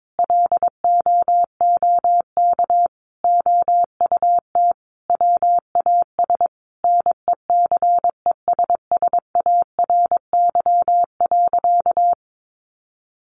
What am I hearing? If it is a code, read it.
Morse code, heard as a keyed tone, about 22 wpm, 700 Hz. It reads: LOOK OUT WAS NECESSARY.
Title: Sentences from Top 1500 Words 22wpm